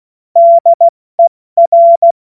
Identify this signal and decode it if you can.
Morse code, heard as a keyed tone, about 16 wpm, 680 Hz. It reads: DER
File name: Morsekurs / Witze / Witz 01. Februar 2022 70 F80